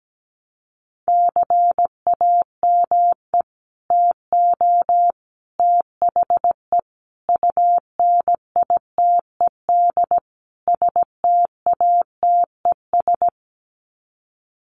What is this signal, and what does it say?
Morse code, heard as a keyed tone, about 17 wpm, 700 Hz. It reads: CAME TO THE UNITED STATES
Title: Sets of 5 Words Encore2 17wpm